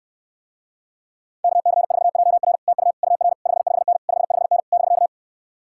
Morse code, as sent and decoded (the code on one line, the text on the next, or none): -.- -.-. ....- .--- .-. / - ..- / ..- .-. / ..... ..... -. / ..... ..... -. / -...-.-
KC4JR TU UR 55N 55N <BK>